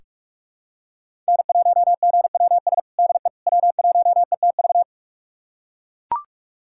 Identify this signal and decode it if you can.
Morse code, heard as a keyed tone, about 41 wpm, 700 Hz. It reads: N1GWS DE W1ETV
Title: USA Callsign DE Callsign - 5 Characters - 45wpm